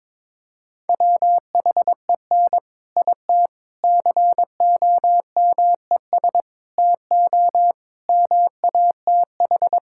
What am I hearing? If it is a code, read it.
Morse code, heard as a keyed tone, about 23 wpm, 700 Hz. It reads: WHEN IT COMES TO MATH
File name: practice / 22wpm / Sentences with 3-Word Phrases - 22wpm